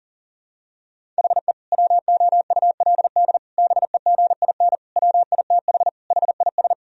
Morse code, heard as a keyed tone, about 37 wpm, 700 Hz. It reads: HE WOULD BEGIN WITH HIS
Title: Sentences from Top 1000 Words - Review Part 1 - 40wpm